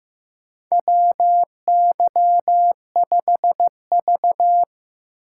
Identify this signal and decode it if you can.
Morse code, heard as a keyed tone, about 15 wpm, 700 Hz. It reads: W Y 5 V
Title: USA Call Signs - 6x Character Spacing - 15wpm